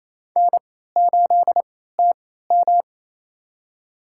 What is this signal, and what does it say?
Morse code, heard as a keyed tone, about 28 wpm, spaced as for 16 wpm, 700 Hz. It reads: N8TM
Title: USA Call Signs - 3x Character Spacing - 28wpm